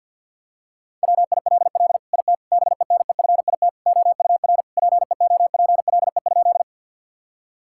Morse code, heard as a keyed tone, about 50 wpm, 700 Hz. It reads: WILL IT BENEFIT OUR PEOPLE?
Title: Sentences from Top 1000 Words - Review Part 6 - 50wpm